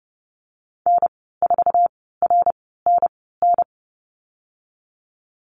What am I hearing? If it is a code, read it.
Morse code, heard as a keyed tone, about 30 wpm, spaced as for 17 wpm, 700 Hz. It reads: N4RNN
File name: Lesson 14g - Number 2 - Call Signs 30wpm